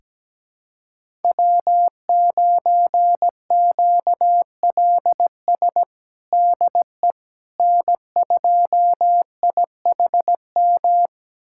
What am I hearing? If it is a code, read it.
Morse code, heard as a keyed tone, about 17 wpm, 700 Hz. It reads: W9QLS DE N2IHM
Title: USA Callsign DE Callsign - 5 Characters - 17wpm